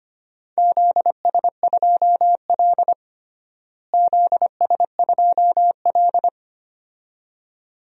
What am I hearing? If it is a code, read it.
Morse code, heard as a keyed tone, about 25 wpm, 700 Hz. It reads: ZS2L ZS2L
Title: DXCC-100 Call Signs - 4 Characters - Repeated 2x - 25wpm